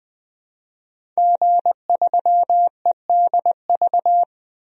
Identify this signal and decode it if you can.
Morse code, heard as a keyed tone, about 20 wpm, 700 Hz. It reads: G3EDV
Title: DXCC-100 Call Signs - 5 Characters - 20wpm